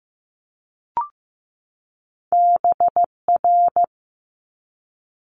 Morse code, heard as a keyed tone, about 15 wpm, 700 Hz. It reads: BR
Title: Multi-letter Phonemes 15wpm